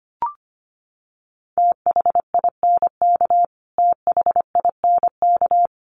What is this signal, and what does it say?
Morse code, heard as a keyed tone, about 25 wpm, 700 Hz. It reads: THINK THINK
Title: Top 300 Words Repeated 25wpm